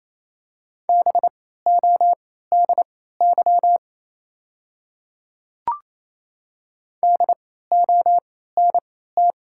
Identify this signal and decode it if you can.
Morse code, heard as a keyed tone, about 28 wpm, spaced as for 16 wpm, 700 Hz. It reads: BODY DONT
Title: 3x Character Spacing - Top 500 Words - 28wpm